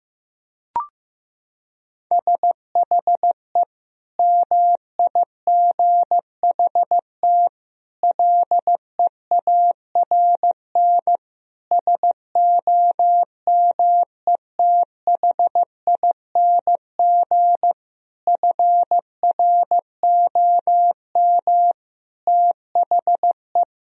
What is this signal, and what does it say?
Morse code, heard as a keyed tone, about 15 wpm, 700 Hz. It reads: SHE MIGHT LEARN SOMETHING FROM THE
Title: Sentences from Top 1000 Words - Review Part 4 - 15wpm